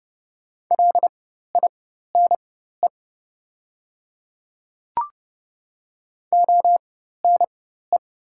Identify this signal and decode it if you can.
Morse code, heard as a keyed tone, about 30 wpm, spaced as for 14 wpm, 700 Hz. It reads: LINE ONE
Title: 4x Character Spacing - Top 200 Words - 30wpm